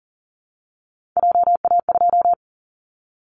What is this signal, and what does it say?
Morse code, heard as a keyed tone, about 40 wpm, 700 Hz. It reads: JA2
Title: Callsign Prefixes for Top 25 DXCC - 40wpm